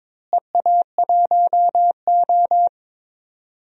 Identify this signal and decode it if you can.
Morse code, heard as a keyed tone, about 22 wpm, 700 Hz. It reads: EA1O
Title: DXCC-100 Call Signs - 4 Characters - Repeated 2x - 22wpm